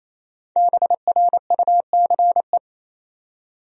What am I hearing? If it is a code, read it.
Morse code, heard as a keyed tone, about 28 wpm, 700 Hz. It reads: BRUCE
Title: USA Common Names - Max 5 Letters - 28wpm